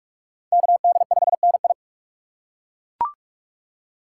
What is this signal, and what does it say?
Morse code, heard as a keyed tone, about 40 wpm, 700 Hz. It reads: KD5NI